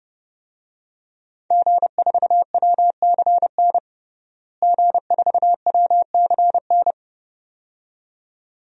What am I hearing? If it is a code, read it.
Morse code, heard as a keyed tone, about 32 wpm, 700 Hz. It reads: G4WCN G4WCN
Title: DXCC-100 Call Signs - 5 Characters - Repeated 2x - 30wpm